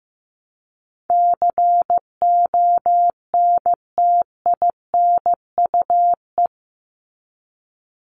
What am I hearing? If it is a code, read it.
Morse code, heard as a keyed tone, about 15 wpm, 700 Hz. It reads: CONTINUE
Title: Top 751-1000 Words 15wpm